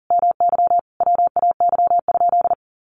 Morse code, heard as a keyed tone, about 37 wpm, 700 Hz. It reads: MY WAY?
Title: Sentences from Top 200 Words 40wpm